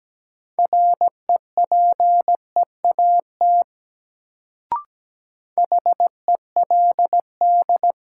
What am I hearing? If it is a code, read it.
Morse code, heard as a keyed tone, about 17 wpm, 700 Hz. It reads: REPEAT HELD
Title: Sentences from Top 1600 Words - New Word Drill Alt - 17wpm